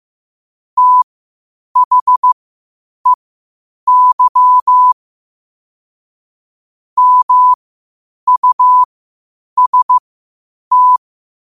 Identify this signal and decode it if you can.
Morse code, heard as a keyed tone, about 15 wpm, spaced as for 8 wpm, 1000 Hz. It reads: THEY MUST
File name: qotdfiles / 8wpm